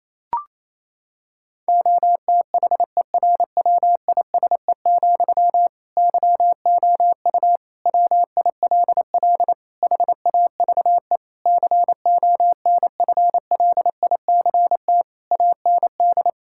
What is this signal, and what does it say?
Morse code, heard as a keyed tone, about 28 wpm, 700 Hz. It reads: OTHERWISE, YOU WILL HAVE CONFLICT AND
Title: Sentences from Top 1600 Words 28wpm